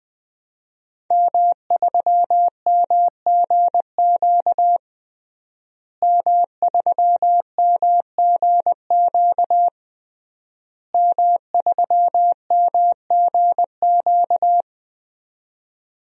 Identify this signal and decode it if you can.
Morse code, heard as a keyed tone, about 21 wpm, 700 Hz. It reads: M3MGQ M3MGQ M3MGQ
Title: DXCC-100 Call Signs - 5 Characters - Repeated 3x - 20wpm